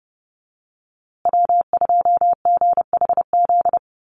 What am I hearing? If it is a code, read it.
Morse code, heard as a keyed tone, about 30 wpm, 700 Hz. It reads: W2GHZ
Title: USA Call Signs Repeated 2x 30wpm